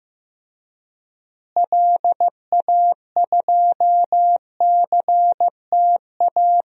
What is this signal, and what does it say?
Morse code, heard as a keyed tone, about 15 wpm, 700 Hz. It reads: LA2CTA
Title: DXCC-100 Call Signs - 6 Characters - Repeated 2x - 15wpm